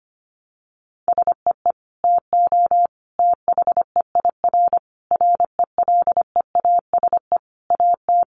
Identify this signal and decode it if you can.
Morse code, heard as a keyed tone, about 25 wpm, 700 Hz. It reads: SEE TO THEIR RELEASE AT
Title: Sentences from Top 1100 Words 25wpm